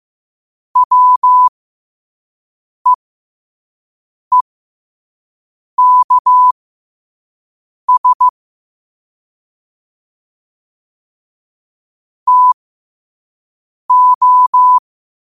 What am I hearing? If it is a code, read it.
Morse code, heard as a keyed tone, about 15 wpm, spaced as for 5 wpm, 1000 Hz. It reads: WEEKS TO